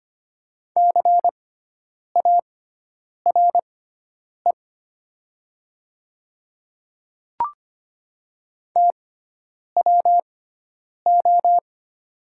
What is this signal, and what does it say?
Morse code, heard as a keyed tone, about 25 wpm, spaced as for 9 wpm, 700 Hz. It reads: CARE TWO
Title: 6x Character Spacing - Top 300 Words - 25wpm